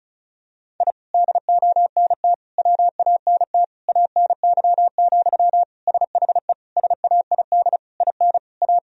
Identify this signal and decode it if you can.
Morse code, heard as a keyed tone, about 36 wpm, 700 Hz. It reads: I DONT WANT ANY, SHE SAID IN A
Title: Sentences from Top 1000 Words - Review Part 1 - 35wpm